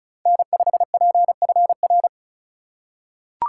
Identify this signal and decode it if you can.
Morse code, heard as a keyed tone, about 36 wpm, 700 Hz. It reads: N5PFR
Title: USA Call Signs - 2x Character Spacing - 35wpm